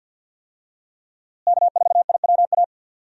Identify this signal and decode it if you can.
Morse code, heard as a keyed tone, about 50 wpm, 700 Hz. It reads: K4IWA
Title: USA Call Signs - 5 Characters - Repeated 3x - 50wpm